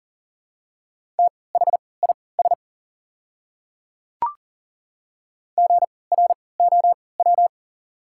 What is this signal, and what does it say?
Morse code, heard as a keyed tone, about 40 wpm, spaced as for 23 wpm, 700 Hz. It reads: THIS GROW
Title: 3x Character Spacing - Top 500 Words - 40wpm